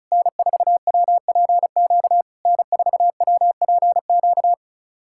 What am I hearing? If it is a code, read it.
Morse code, heard as a keyed tone, about 37 wpm, 700 Hz. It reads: N4WPQ N4WPQ
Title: SOTA QSO Practice 35wpm